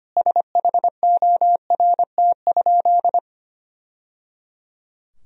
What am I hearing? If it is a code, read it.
Morse code, heard as a keyed tone, about 25 wpm, 700 Hz. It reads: SHORT?
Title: Sentences from Top 1000 Words - Review Part 2 - 25wpm